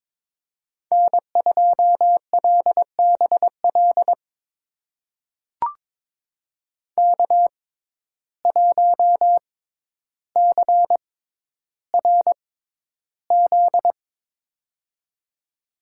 Morse code, heard as a keyed tone, about 22 wpm, spaced as for 11 wpm, 700 Hz. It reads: N2LBL K1CRZ
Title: USA Call Signs - 5 Characters - 6x Character Spacing - 22wpm